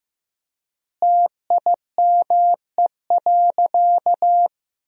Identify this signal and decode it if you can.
Morse code, heard as a keyed tone, about 15 wpm, 700 Hz. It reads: TIME.
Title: 3x Word-Spacing - Sentences from Top 100 Words - 15wpm